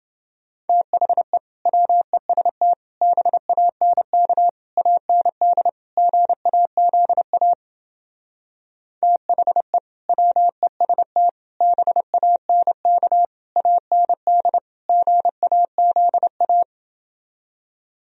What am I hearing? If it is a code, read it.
Morse code, heard as a keyed tone, about 28 wpm, 700 Hz. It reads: THE WEST BANK AND GAZA THE WEST BANK AND GAZA